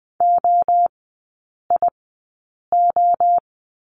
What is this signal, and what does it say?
Morse code, heard as a keyed tone, about 20 wpm, spaced as for 8 wpm, 700 Hz. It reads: OIO
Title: 047 Lesson 9e - Number 4 - Three Characters 20wpm